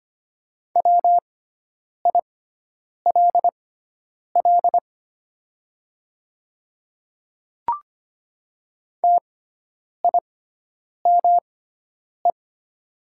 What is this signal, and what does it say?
Morse code, heard as a keyed tone, about 25 wpm, spaced as for 9 wpm, 700 Hz. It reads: WILL TIME
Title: 6x Character Spacing - Top 200 Words - 25wpm